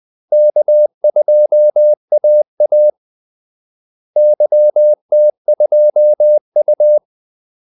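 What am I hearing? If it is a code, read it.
Morse code, heard as a keyed tone, about 20 wpm, 600 Hz. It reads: K2AA YT2U
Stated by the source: Morse call-sign practice at 20 wpm.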